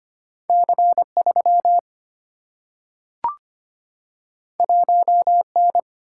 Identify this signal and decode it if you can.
Morse code, heard as a keyed tone, about 25 wpm, 700 Hz. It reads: C3 1N
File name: Two Letter-Number Combinations - 25wpm